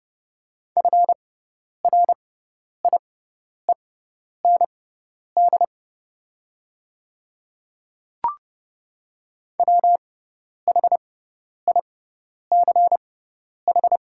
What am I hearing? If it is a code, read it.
Morse code, heard as a keyed tone, about 30 wpm, spaced as for 10 wpm, 700 Hz. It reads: FRIEND WHICH
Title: 6x Character Spacing - Top 300 Words - 30wpm